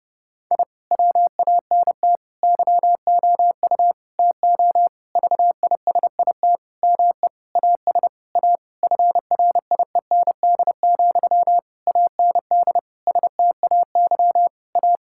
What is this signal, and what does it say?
Morse code, heard as a keyed tone, about 30 wpm, 700 Hz. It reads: I WANT YOU TO VISIT ME AS A FRIEND, AND STAY A
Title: Sentences from Top 1000 Words - Review Part 6 - 30wpm